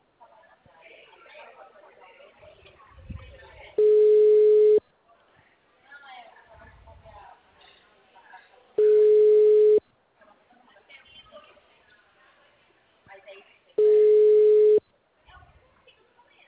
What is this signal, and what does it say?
Ringback tone (Europe, 425 Hz), with speech.